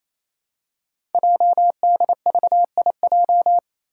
Morse code, heard as a keyed tone, about 27 wpm, 700 Hz. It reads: JDVIJ
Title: Code Groups - 5 Letter:Numbers - 28wpm